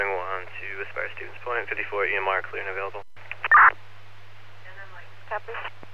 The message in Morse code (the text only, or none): none